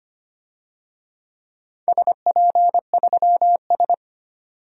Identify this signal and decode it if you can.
Morse code, heard as a keyed tone, about 25 wpm, 700 Hz. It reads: SP3S
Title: DXCC-100 Call Signs - 4 Characters - 25wpm